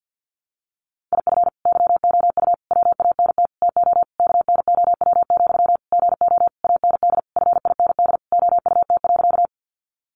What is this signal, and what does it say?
Morse code, heard as a keyed tone, about 48 wpm, 700 Hz. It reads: IF YOU WANT TO KNOW, GO AND FIND OUT.
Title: Sentences from Top 200 Words 50wpm